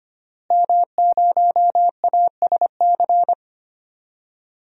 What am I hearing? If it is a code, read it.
Morse code, heard as a keyed tone, about 25 wpm, 700 Hz. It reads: M0ASC
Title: DXCC-100 Call Signs - 5 Characters - Repeated 2x - 25wpm